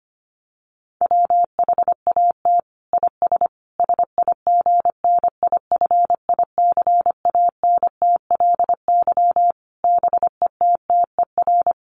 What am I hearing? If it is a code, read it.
Morse code, heard as a keyed tone, about 25 wpm, 700 Hz. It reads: WHAT IS SIGNIFICANTLY BETTER
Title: Sentences from Top 1500 Words 25wpm